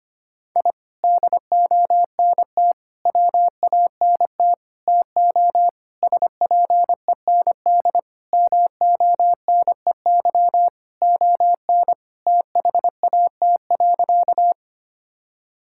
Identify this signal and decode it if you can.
Morse code, heard as a keyed tone, about 25 wpm, 700 Hz. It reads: I DONT WANT TO SPEND MONEY ON THAT.
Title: Sentences from Top 1000 Words - Review Part 5 - 25wpm